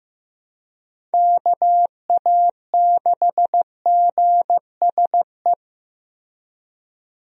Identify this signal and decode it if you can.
Morse code, heard as a keyed tone, about 15 wpm, 700 Hz. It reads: KA6GSE
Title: USA Call Signs - 6 Characters - Repeated 3x - 15wpm